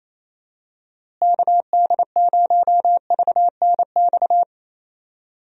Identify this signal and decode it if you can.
Morse code, heard as a keyed tone, about 28 wpm, 700 Hz. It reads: KD0VNX
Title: USA Call Signs - 6 Characters - Repeated 2x - 28wpm